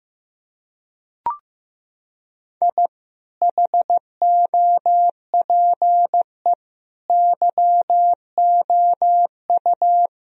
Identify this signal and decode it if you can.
Morse code, heard as a keyed tone, about 15 wpm, 700 Hz. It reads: I HOPE YOU
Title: Sentences from Top 1000 Words - Review Part 1 - 15wpm